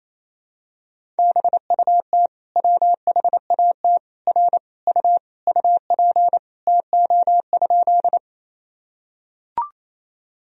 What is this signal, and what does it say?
Morse code, heard as a keyed tone, about 28 wpm, 700 Hz. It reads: BUT WHAT R U UP TO?
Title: Sentences from Top 100 Words and Abbreviations 28wpm